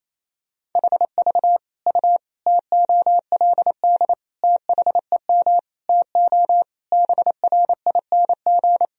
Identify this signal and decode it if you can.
Morse code, heard as a keyed tone, about 28 wpm, 700 Hz. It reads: HV U TOLD THEM TO BRING
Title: Sentences from Top 500 Words and Abbreviations 28wpm